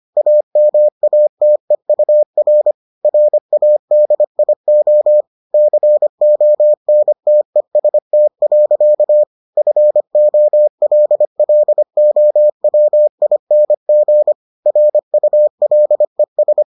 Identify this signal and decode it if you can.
Morse code, heard as a keyed tone, about 25 wpm, 600 Hz. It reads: AMATEUR RADIO CONTEST. FOLLOWING RULES